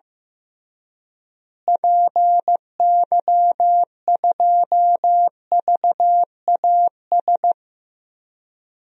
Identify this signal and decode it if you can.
Morse code, heard as a keyed tone, about 15 wpm, 700 Hz. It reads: PY2VAS
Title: DXCC-100 Call Signs - 6 Characters - Repeated 2x - 15wpm